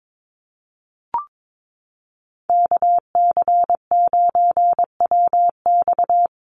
Morse code, heard as a keyed tone, about 22 wpm, 700 Hz. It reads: KC9WX
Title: USA Call Signs 22wpm